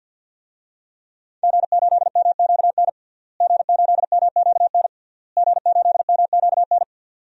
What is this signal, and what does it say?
Morse code, heard as a keyed tone, about 50 wpm, 700 Hz. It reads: G8MQN G8MQN G8MQN
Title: DXCC-100 Call Signs - 5 Characters - Repeated 3x - 50wpm